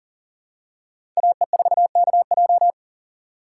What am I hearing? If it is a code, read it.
Morse code, heard as a keyed tone, about 40 wpm, 700 Hz. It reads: AE4KJ
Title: USA Call Signs - 6x Character Spacing - 40wpm